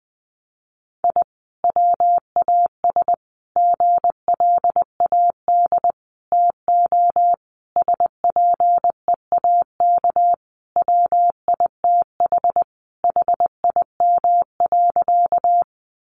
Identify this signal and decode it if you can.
Morse code, heard as a keyed tone, about 20 wpm, 700 Hz. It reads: I WAS GLAD TO SPEAK WITH HIM.
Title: Sentences from Top 1100 Words 20wpm